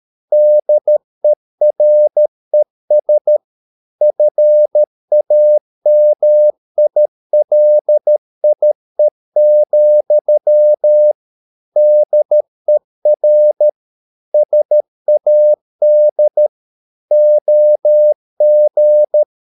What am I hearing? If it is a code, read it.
Morse code, heard as a keyed tone, about 13 wpm, 600 Hz. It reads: DERES FAMILIE, DER SAD OG
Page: Ferie 13 wpm